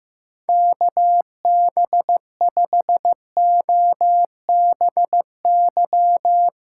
Morse code, heard as a keyed tone, about 15 wpm, 700 Hz. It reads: KB5OBY
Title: USA Callsign DE Callsign 15wpm